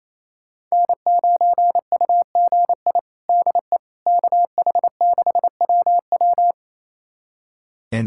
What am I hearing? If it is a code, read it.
Morse code, heard as a keyed tone, about 28 wpm, 700 Hz. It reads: N9UGI DE KH6WW
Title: USA Callsign DE Callsign - 5 Characters - 28wpm